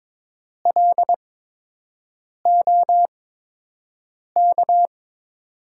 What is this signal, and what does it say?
Morse code, heard as a keyed tone, about 22 wpm, spaced as for 6 wpm, 700 Hz. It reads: LOK
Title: USA Call Signs - 6 Characters - 8x Character Spacing - 22wpm